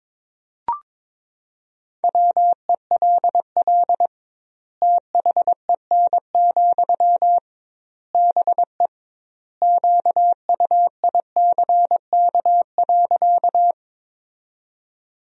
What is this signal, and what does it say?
Morse code, heard as a keyed tone, about 22 wpm, 700 Hz. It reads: WELL THEN, BE QUICK.
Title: 2x Word-Spacing - Sentences from Top 500 Words - 22wpm